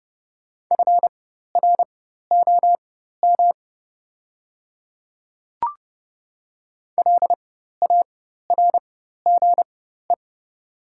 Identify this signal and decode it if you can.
Morse code, heard as a keyed tone, about 30 wpm, spaced as for 14 wpm, 700 Hz. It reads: FROM LARGE
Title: 4x Character Spacing - Top 500 Words - 30wpm